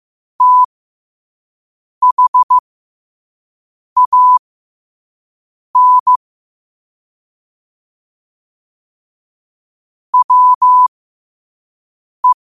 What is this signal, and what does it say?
Morse code, heard as a keyed tone, about 15 wpm, spaced as for 5 wpm, 1000 Hz. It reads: THAN WE
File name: qotdfiles / 5wpm